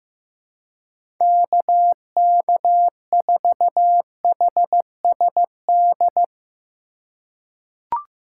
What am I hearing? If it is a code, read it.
Morse code, heard as a keyed tone, about 15 wpm, 700 Hz. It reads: KK4HSD
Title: USA Call Signs - 3x Character Spacing - 15wpm